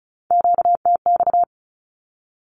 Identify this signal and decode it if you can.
Morse code, heard as a keyed tone, about 36 wpm, 700 Hz. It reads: QTX
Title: Q-Codes with Spoken Meaning - 35wpm